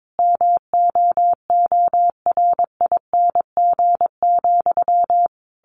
Morse code, heard as a keyed tone, about 22 wpm, 700 Hz. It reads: MOORING,
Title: A Sound Of Thunder 22wpm